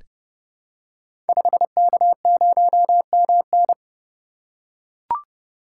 Morse code, heard as a keyed tone, about 30 wpm, 700 Hz. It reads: 5K0MN
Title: Code Groups - 5 Letter:Numbers - 30wpm